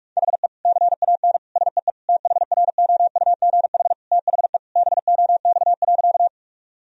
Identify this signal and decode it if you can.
Morse code, heard as a keyed tone, about 46 wpm, 700 Hz. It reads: HE CAN SEE THROUGH THE BOX.
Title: Sentences from Top 1000 Words - Review Part 2 - 45wpm